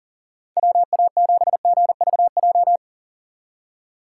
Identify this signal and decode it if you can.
Morse code, heard as a keyed tone, about 40 wpm, 700 Hz. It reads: WA7GVJ
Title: USA Call Signs - 6x Character Spacing - 40wpm